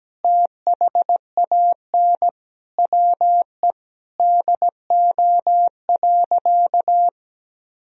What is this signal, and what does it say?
Morse code, heard as a keyed tone, about 17 wpm, 700 Hz. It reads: THAN WE DO.
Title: Sentences from Top 100 Words and Abbreviations 17wpm